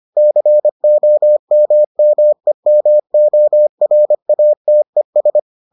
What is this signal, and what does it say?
Morse code, heard as a keyed tone, about 25 wpm, 600 Hz. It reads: COMMEMORATES